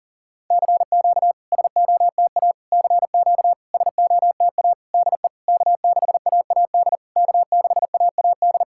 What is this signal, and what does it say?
Morse code, heard as a keyed tone, about 40 wpm, 700 Hz. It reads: CQ SOTA CQ SOTA DE K6AAD K6AAD
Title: SOTA QSO Practice 40wpm